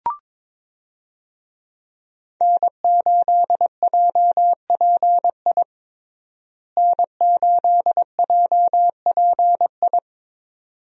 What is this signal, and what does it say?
Morse code, heard as a keyed tone, about 21 wpm, 700 Hz. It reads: N8JPI N8JPI
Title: USA Call Signs - 5 Characters - Repeated 2x - 22wpm